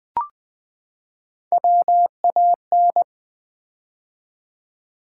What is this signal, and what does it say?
Morse code, heard as a keyed tone, about 20 wpm, 700 Hz. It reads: WAN
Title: Build A Word - 3 Letters At A Time - 6-Letter Words - 20wpm